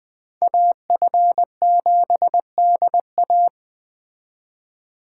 Morse code, heard as a keyed tone, about 20 wpm, 700 Hz. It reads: AF7DA